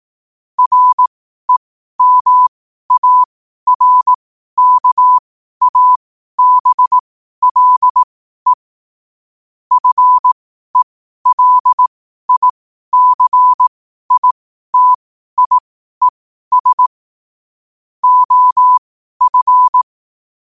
Morse code, heard as a keyed tone, about 18 wpm, spaced as for 12 wpm, 1000 Hz. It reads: REMARKABLE FELICITIES OF
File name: qotdfiles / 12wpm